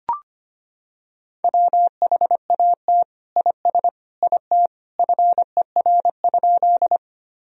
Morse code, heard as a keyed tone, about 25 wpm, 700 Hz. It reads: WHAT IS IT FER?
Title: Sentences from Top 100 Words and Abbreviations 25wpm